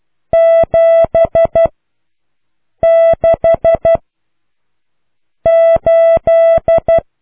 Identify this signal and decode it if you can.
Morse code, heard as a keyed tone, about 12 wpm, 660 Hz. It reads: 7 6 8